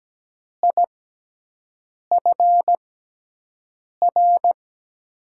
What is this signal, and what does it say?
Morse code, heard as a keyed tone, about 17 wpm, spaced as for 6 wpm, 700 Hz. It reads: IFR
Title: USA Call Signs - 6x Character Spacing - 17wpm